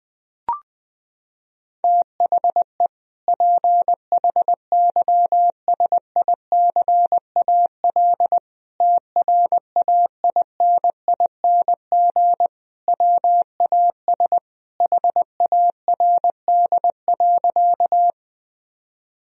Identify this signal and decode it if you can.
Morse code, heard as a keyed tone, about 21 wpm, 700 Hz. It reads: THE PHYSICAL TRAINING WAS HARD.
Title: Sentences from Top 1000 Words - Review Part 4 - 20wpm